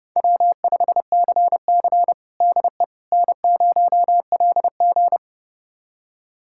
Morse code, heard as a keyed tone, about 30 wpm, 700 Hz. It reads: W5CC DE N0LG
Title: USA Callsign DE Callsign - 4 Characters - 30wpm